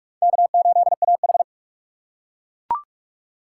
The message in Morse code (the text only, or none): K8AH